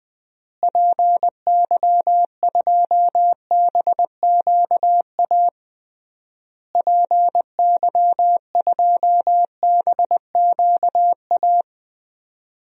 Morse code, heard as a keyed tone, about 21 wpm, 700 Hz. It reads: PY2BQA PY2BQA
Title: DXCC-100 Call Signs - 6 Characters - Repeated 3x - 20wpm